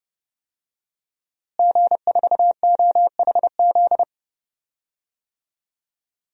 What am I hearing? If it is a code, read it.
Morse code, heard as a keyed tone, about 32 wpm, 700 Hz. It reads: G4OHZ
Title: DXCC-100 Call Signs - 5 Characters - Repeated 3x - 30wpm